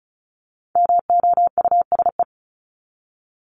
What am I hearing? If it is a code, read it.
Morse code, heard as a keyed tone, about 34 wpm, 700 Hz. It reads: MOUSE
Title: Top 501-750 Words 35wpm